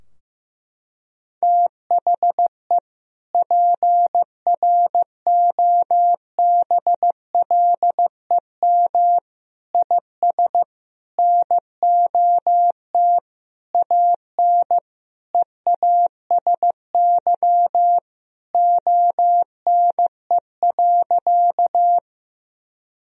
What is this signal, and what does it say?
Morse code, heard as a keyed tone, about 15 wpm, 700 Hz. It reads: THE PROBLEM IS NOT AN EASY ONE.
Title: Sentences from Top 1000 Words - Review Part 2 - 15wpm